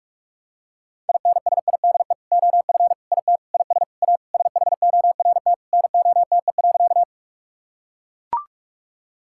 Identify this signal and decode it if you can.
Morse code, heard as a keyed tone, about 42 wpm, 700 Hz. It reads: INSIDE OF IT IS A SHORT NOTE.
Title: Sentences from Top 1000 Words - Review Part 6 - 45wpm